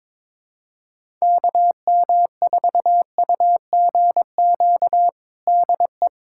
Morse code, heard as a keyed tone, about 22 wpm, 700 Hz. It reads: KM4UGQ DE
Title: USA Callsign DE Callsign 22wpm